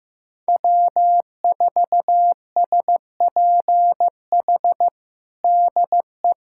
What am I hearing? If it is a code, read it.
Morse code, heard as a keyed tone, about 15 wpm, 700 Hz. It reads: W4SPH DE
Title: USA Callsign DE Callsign - 5 Characters - 15wpm